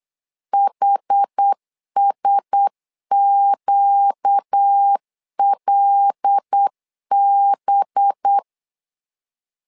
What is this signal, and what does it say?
Morse code, heard as a keyed tone, about 8 wpm, 800 Hz. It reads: HSQLB
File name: LEZIONI / 5 LBVC / 5 LB 8wpm 2816